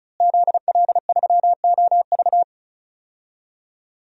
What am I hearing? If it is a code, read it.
Morse code, heard as a keyed tone, about 34 wpm, 700 Hz. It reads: ZL3OV
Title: DXCC-100 Call Signs - 5 Characters - Repeated 3x - 35wpm